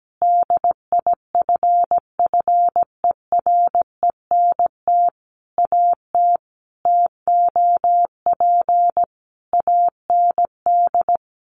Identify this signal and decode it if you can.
Morse code, heard as a keyed tone, about 17 wpm, 700 Hz. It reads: DIFFERENT AT TOP AND
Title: Sentences from Top 1400 Words 17wpm